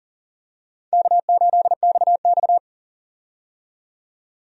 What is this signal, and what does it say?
Morse code, heard as a keyed tone, about 40 wpm, 700 Hz. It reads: K8XX
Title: USA Call Signs - 4 Characters - 40wpm